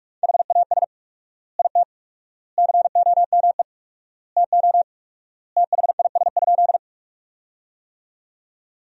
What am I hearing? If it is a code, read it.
Morse code, heard as a keyed tone, about 46 wpm, 700 Hz. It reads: HAS IT COME TO THIS?
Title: 4x Word-Spacing - Sentences from Top 100 Words - 45wpm